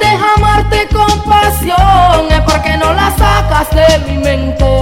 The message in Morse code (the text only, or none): none